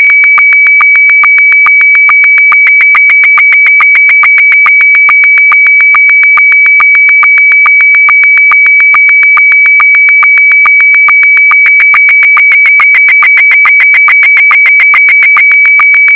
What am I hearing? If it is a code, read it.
SSTV (slow-scan television), Scottie 1.